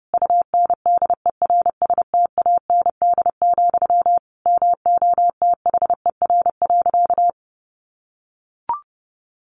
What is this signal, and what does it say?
Morse code, heard as a keyed tone, about 30 wpm, 700 Hz. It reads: UNDERSTAND, MOTHER.
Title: Sentences from Top 600 Words 30wpm